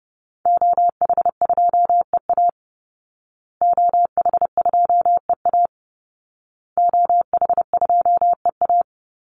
Morse code, heard as a keyed tone, about 30 wpm, 700 Hz. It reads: OH2EA OH2EA OH2EA
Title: Call Signs Repeated 2x - 30wpm